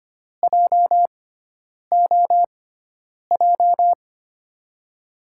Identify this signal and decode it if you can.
Morse code, heard as a keyed tone, about 25 wpm, spaced as for 9 wpm, 700 Hz. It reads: JOJ
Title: USA Call Signs - 6x Character Spacing - 25wpm